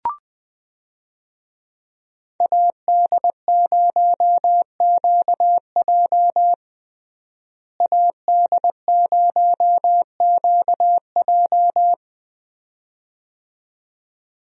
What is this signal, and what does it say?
Morse code, heard as a keyed tone, about 20 wpm, 700 Hz. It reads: AD0QJ AD0QJ
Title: USA Call Signs - 5 Characters - Repeated 2x - 20wpm